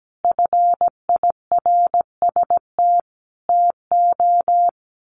Morse code, heard as a keyed tone, about 17 wpm, 700 Hz. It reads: FIRST TO
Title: Sets of 5 Words Encore 17wpm